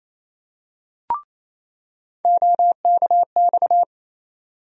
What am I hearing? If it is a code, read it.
Morse code, heard as a keyed tone, about 28 wpm, 700 Hz. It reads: OKX
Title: USA Call Sign Postfixes - 3 Letters 28wpm